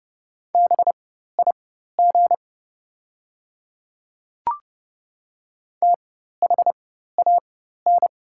Morse code, heard as a keyed tone, about 30 wpm, spaced as for 14 wpm, 690 Hz. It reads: BIG THAN
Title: 4x Character Spacing - Top 200 Words - 30wpm